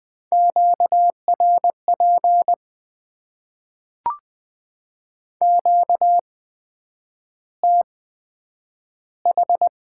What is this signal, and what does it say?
Morse code, heard as a keyed tone, about 20 wpm, spaced as for 9 wpm, 700 Hz. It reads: QRP QTH
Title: Top 90 QSO Elements - 8x Character Spacing - 20wpm